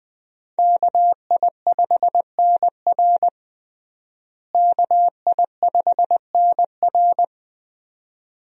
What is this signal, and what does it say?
Morse code, heard as a keyed tone, about 20 wpm, 700 Hz. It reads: KI5NR KI5NR
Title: USA Call Signs - 5 Characters - Repeated 2x - 20wpm